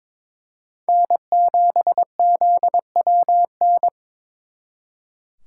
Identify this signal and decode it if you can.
Morse code, heard as a keyed tone, about 26 wpm, 700 Hz. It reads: N7ZWN
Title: Speed Racing - USA 1x3 Call Signs 22wpm